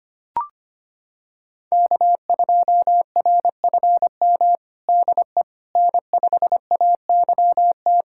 Morse code, heard as a keyed tone, about 25 wpm, 700 Hz. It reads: K2RFM DE N5AYT
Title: USA Callsign DE Callsign - 5 Characters - 25wpm